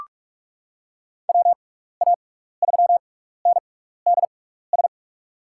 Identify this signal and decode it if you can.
Morse code, heard as a keyed tone, about 41 wpm, spaced as for 15 wpm, 700 Hz. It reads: WA3NDS
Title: USA Call Signs - 6 Characters - 6x Character Spacing - 45wpm